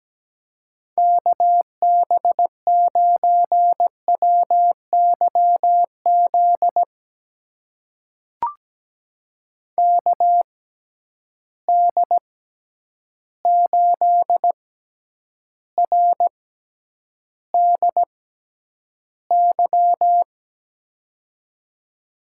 Morse code, heard as a keyed tone, about 17 wpm, spaced as for 9 wpm, 700 Hz. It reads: KB9WYZ KD8RDY
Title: USA Call Signs - 6 Characters - 6x Character Spacing - 17wpm